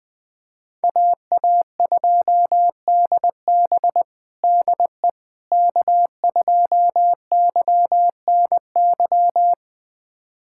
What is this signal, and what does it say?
Morse code, heard as a keyed tone, about 20 wpm, 700 Hz. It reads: AA2DB DE K2YNY
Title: USA Callsign DE Callsign - 5 Characters - 20wpm